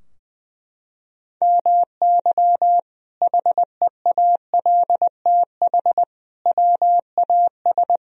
Morse code, heard as a keyed tone, about 20 wpm, 700 Hz. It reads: MY HEALTH WAS
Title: Sentences from Top 1000 Words - Review Part 3 - 20wpm